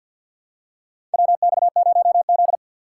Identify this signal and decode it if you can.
Morse code, heard as a keyed tone, about 48 wpm, 700 Hz. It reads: WX0Z
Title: K1USN SST QSO Practice - 50wpm